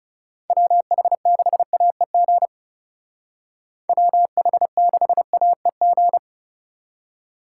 Morse code, heard as a keyed tone, about 34 wpm, 700 Hz. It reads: WH6AEG WH6AEG
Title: Speed Racing - USA 2x3 Call Signs 30wpm